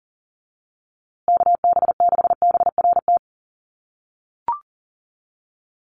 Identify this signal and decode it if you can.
Morse code, heard as a keyed tone, about 40 wpm, 700 Hz. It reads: KB6BRT
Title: USA Call Signs Repeated 3x 40wpm